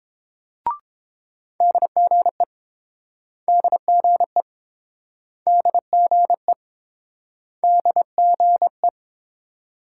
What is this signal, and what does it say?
Morse code, heard as a keyed tone, about 30 wpm, 700 Hz. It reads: DGE DGE DGE DGE